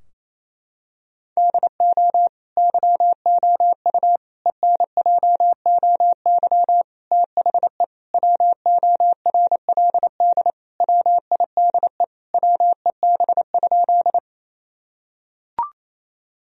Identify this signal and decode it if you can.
Morse code, heard as a keyed tone, about 28 wpm, 700 Hz. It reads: DO YOU ENJOY THE WORLD WIDE WEB?
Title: Sentences from Top 1000 Words - Review Part 4 - 28wpm